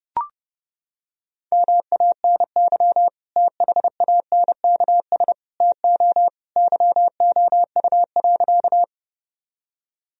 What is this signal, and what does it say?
Morse code, heard as a keyed tone, about 30 wpm, 700 Hz. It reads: MANY THANKS TO YOU.
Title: Sentences from Top 1000 Words - Review Part 3 - 30wpm